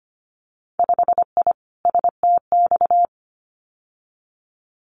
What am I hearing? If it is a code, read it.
Morse code, heard as a keyed tone, about 25 wpm, 690 Hz. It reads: 5I STX
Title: Winter Field Day Exchanges Encore 25wpm